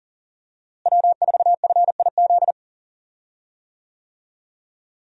Morse code, heard as a keyed tone, about 40 wpm, 700 Hz. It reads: W4FIZ
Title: USA Call Signs - 5 Characters - Repeated 2x - 40wpm